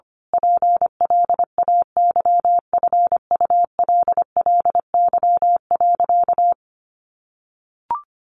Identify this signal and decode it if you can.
Morse code, heard as a keyed tone, about 25 wpm, 700 Hz. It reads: PLAYFULLY.
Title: A Sound Of Thunder 25wpm